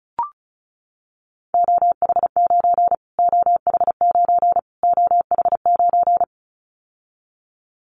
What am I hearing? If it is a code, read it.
Morse code, heard as a keyed tone, about 34 wpm, 700 Hz. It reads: OH9 OH9 OH9
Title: Callsign Prefixes for Top 25 DXCC - Repeated 3x - 35wpm